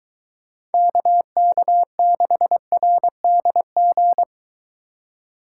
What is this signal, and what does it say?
Morse code, heard as a keyed tone, about 23 wpm, 700 Hz. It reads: KK6RDG